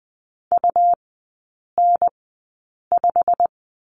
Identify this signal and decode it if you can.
Morse code, heard as a keyed tone, about 20 wpm, spaced as for 8 wpm, 690 Hz. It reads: UN5
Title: 166 Lesson 27e - Number 7 - Three Characters 20wpm